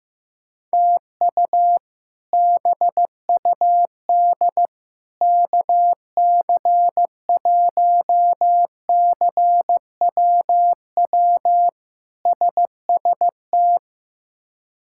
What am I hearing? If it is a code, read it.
Morse code, heard as a keyed tone, about 15 wpm, 700 Hz. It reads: TU BUD KC1CWW SST
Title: K1USN SST QSO Practice - 15wpm